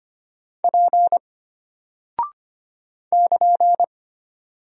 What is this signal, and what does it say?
Morse code, heard as a keyed tone, about 25 wpm, 700 Hz. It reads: P <KN>
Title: Single Character (Letter:Number:Punctuation:Prosign) Alt 25wpm